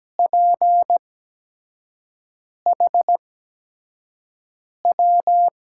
Morse code, heard as a keyed tone, about 17 wpm, spaced as for 5 wpm, 700 Hz. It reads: PHW
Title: USA Call Signs - 6 Characters - 8x Character Spacing - 17wpm